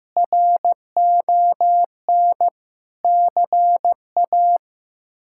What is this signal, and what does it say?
Morse code, heard as a keyed tone, about 15 wpm, 700 Hz. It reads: RON CA
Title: K1USN SST QSO Practice - 15wpm